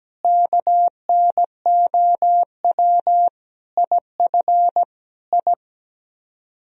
Morse code, heard as a keyed tone, about 17 wpm, 700 Hz. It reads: KNOW IF I
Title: Speed-Racing - Sets of 5 Words 17wpm